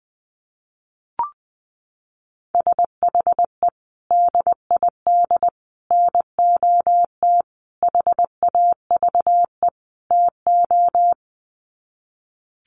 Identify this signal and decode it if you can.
Morse code, heard as a keyed tone, about 20 wpm, 700 Hz. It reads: SHE DID NOT HAVE TO
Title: Sets of 5 Words Encore2 20wpm